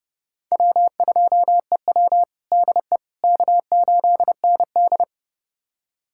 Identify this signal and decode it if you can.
Morse code, heard as a keyed tone, about 32 wpm, 700 Hz. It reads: W2EW DE K8ND
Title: USA Callsign DE Callsign - 4 Characters - 30wpm